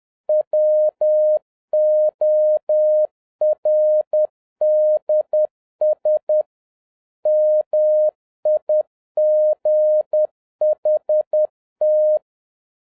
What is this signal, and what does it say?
Morse code, heard as a keyed tone, about 10 wpm, 610 Hz. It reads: WORDS MIGHT
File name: 2011-07-mp3 / n56 BACK FROM A BIG ACCIDENT 10wpm